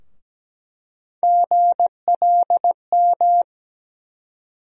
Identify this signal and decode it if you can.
Morse code, heard as a keyed tone, about 17 wpm, 700 Hz. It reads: GLM
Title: USA Call Sign Postfixes - 3 Letters 17wpm